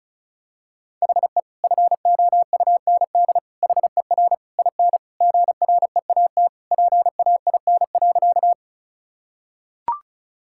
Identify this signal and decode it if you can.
Morse code, heard as a keyed tone, about 36 wpm, 700 Hz. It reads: HE FOUND HER IN GREAT PAIN.
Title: Sentences from Top 1600 Words 35wpm